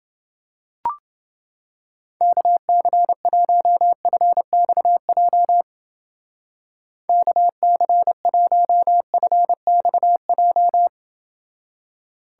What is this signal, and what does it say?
Morse code, heard as a keyed tone, about 28 wpm, 700 Hz. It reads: KC1FXJ KC1FXJ